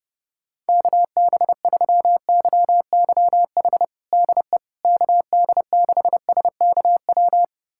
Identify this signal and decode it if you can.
Morse code, heard as a keyed tone, about 30 wpm, 700 Hz. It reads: KB3YYH DE KD6SKW
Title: USA Callsign DE Callsign - 6 Characters - 30wpm